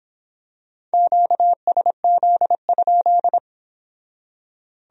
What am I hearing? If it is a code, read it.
Morse code, heard as a keyed tone, about 25 wpm, 700 Hz. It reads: QSZ?